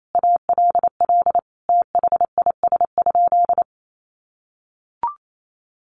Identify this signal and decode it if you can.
Morse code, heard as a keyed tone, about 28 wpm, 690 Hz. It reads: ALL THIS?
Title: Sentences from Top 900 Words 28wpm